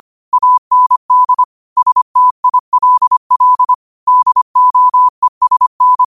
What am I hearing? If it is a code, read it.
Morse code, heard as a keyed tone, about 25 wpm, 1000 Hz. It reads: AND STILL DOESN
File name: 25wpm